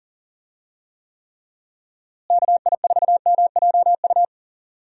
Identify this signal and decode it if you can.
Morse code, heard as a keyed tone, about 40 wpm, 700 Hz. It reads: KI4MJU
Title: USA Call Signs - 6 Characters - Repeated 3x - 40wpm